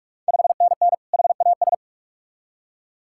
Morse code, heard as a keyed tone, about 46 wpm, 690 Hz. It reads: 5NN HAS
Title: Ontario QSO Party Exchange 45wpm